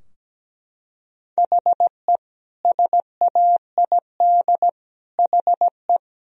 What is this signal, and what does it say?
Morse code, heard as a keyed tone, about 17 wpm, 700 Hz. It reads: HE SAID HE
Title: Sentences from Top 1000 Words - Review Part 4 - 17wpm